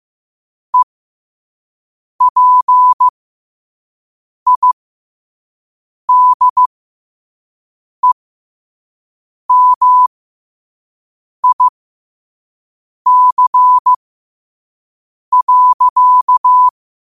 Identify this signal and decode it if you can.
Morse code, heard as a keyed tone, about 15 wpm, spaced as for 5 wpm, 1000 Hz. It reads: EPIDEMIC.
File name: qotdfiles / 5wpm